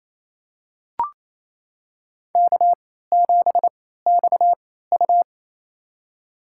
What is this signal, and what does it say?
Morse code, heard as a keyed tone, about 28 wpm, spaced as for 16 wpm, 700 Hz. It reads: K7XU
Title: USA Call Signs - 4 Characters - 3x Character Spacing - 28wpm